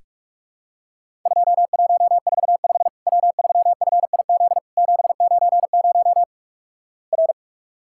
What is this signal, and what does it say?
Morse code, heard as a keyed tone, about 46 wpm, 700 Hz. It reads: 2145 W3FIZ 790 R
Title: Worked All Europe DX Contest - QTC Exchange - 45wpm